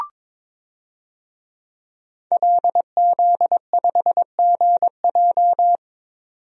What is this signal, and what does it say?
Morse code, heard as a keyed tone, about 22 wpm, 700 Hz. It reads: LZ5GJ
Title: DXCC-100 Call Signs - 5 Characters - 22wpm